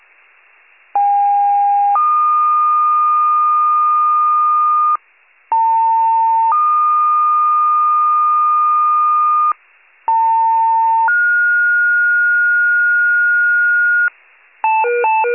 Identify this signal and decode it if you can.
Two-tone page, heard as a one-second tone, then a three-second tone, about 810 then 1220 Hz, about 900 then 1220 Hz, about 900 then 1430 Hz. It is heard from 0.9 to 14.1 s.